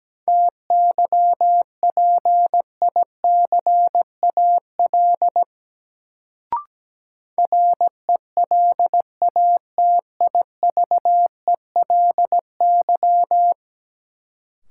Morse code, heard as a keyed tone, about 17 wpm, 700 Hz. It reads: TYPICAL RELATIVELY
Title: Sentences from Top 1100 Words - New Word Drill Alt - 17wpm